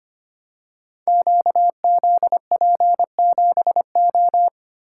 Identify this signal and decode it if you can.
Morse code, heard as a keyed tone, about 25 wpm, 700 Hz. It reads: QZP7O
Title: Code Groups - 5 Letter:Numbers - 25wpm